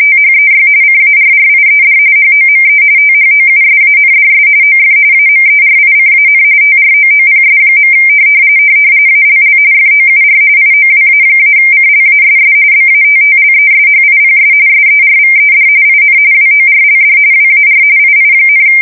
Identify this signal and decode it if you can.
RTTY, about 45.45 baud. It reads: THEY CARRIED HOME THOUSANDS OF AMERICAN TROOPS. ALL FOUR IOWA-CLASS BATTLESHIPS SERVED IN WORLD WAR II,
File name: fsk170 / battleshipnj